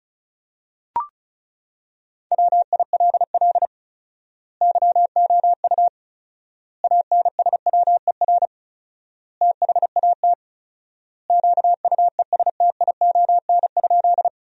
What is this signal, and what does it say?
Morse code, heard as a keyed tone, about 36 wpm, 700 Hz. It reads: WILL YOU ANSWER THAT QUESTION?
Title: 4x Word-Spacing - Sentences from Top 400 Words - 35wpm